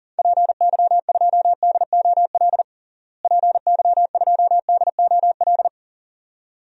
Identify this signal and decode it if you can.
Morse code, heard as a keyed tone, about 40 wpm, 700 Hz. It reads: PY2DOL PY2DOL
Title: DXCC-100 Call Signs - 6 Characters - Repeated 2x - 40wpm